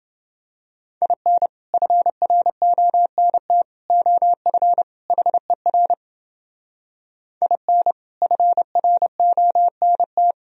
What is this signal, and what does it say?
Morse code, heard as a keyed tone, about 29 wpm, 700 Hz. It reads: IN FRONT OF HER IN FRONT